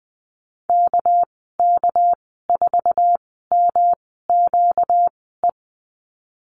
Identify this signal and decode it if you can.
Morse code, heard as a keyed tone, about 19 wpm, spaced as for 14 wpm, 700 Hz. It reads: KK4MQE
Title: 245 Lesson 39g - Letter z - Call Signs 20wpm